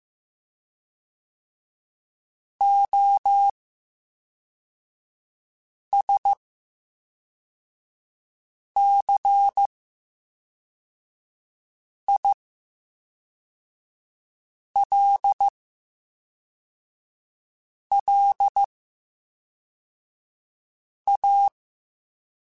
Morse code, heard as a keyed tone, about 15 wpm, spaced as for 3 wpm, 800 Hz. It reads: OSCILLA
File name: Special INGs / Special INGs-mp3s / VerySlowToFaster Ing @10